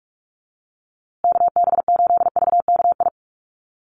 Morse code, heard as a keyed tone, about 44 wpm, 690 Hz. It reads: KB8VKI
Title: USA Call Signs Repeated 3x 45wpm